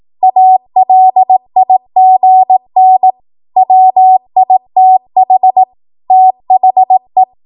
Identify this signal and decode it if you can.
Morse code, heard as a keyed tone, about 18 wpm, 750 Hz. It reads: ALIGN WITH THE
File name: Morse / 18WPM